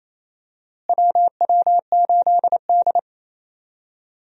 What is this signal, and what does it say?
Morse code, heard as a keyed tone, about 28 wpm, 700 Hz. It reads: WW8D
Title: USA Call Signs - 4 Characters - Repeated 2x - 28wpm